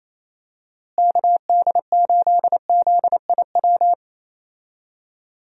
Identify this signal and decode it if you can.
Morse code, heard as a keyed tone, about 28 wpm, 700 Hz. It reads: KD8ZIW
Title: USA Call Signs - 6 Characters - Repeated 2x - 28wpm